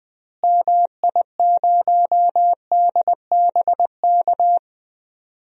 Morse code, heard as a keyed tone, about 20 wpm, 700 Hz. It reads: MI0DBK
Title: DXCC-100 Call Signs - 6 Characters - Repeated 2x - 20wpm